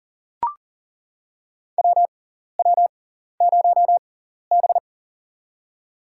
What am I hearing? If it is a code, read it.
Morse code, heard as a keyed tone, about 40 wpm, spaced as for 14 wpm, 700 Hz. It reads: WW0B
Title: USA Call Signs - 6x Character Spacing - 40wpm